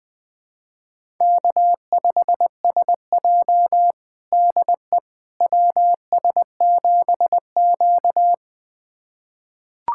Morse code, heard as a keyed tone, about 20 wpm, 700 Hz. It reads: K5SJ DE WS7Q
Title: USA Callsign DE Callsign - 4 Characters - 20wpm